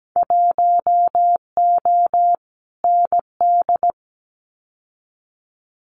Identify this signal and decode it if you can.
Morse code, heard as a keyed tone, about 17 wpm, 700 Hz. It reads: 1O ND
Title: Winter Field Day Exchanges 17wpm